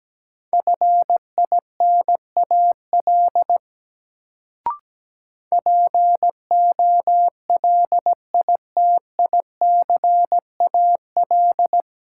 Morse code, heard as a keyed tone, about 17 wpm, 700 Hz. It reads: FINAL POLITICAL
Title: Sentences from Top 800 Words - New Word Drill - 17wpm